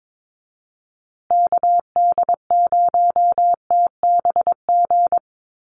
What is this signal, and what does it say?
Morse code, heard as a keyed tone, about 22 wpm, 700 Hz. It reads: KD0TBG
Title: USA Call Signs Repeated 3x 22wpm